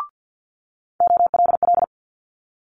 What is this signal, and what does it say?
Morse code, heard as a keyed tone, about 50 wpm, 700 Hz. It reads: OLL
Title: Multi-letter Phonemes Quick Fire 50wpm